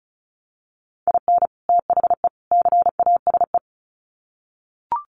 Morse code, heard as a keyed tone, about 34 wpm, 700 Hz. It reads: IN THE CASE
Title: Sets of 3 Words 35wpm